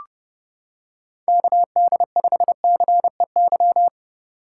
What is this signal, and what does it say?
Morse code, heard as a keyed tone, about 30 wpm, 700 Hz. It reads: KD5CEY